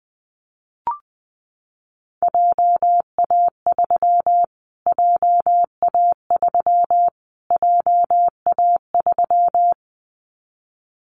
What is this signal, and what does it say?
Morse code, heard as a keyed tone, about 19 wpm, 700 Hz. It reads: JA3 JA3 JA3
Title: Callsign Prefixes for Top 25 DXCC - Repeated 3x - 20wpm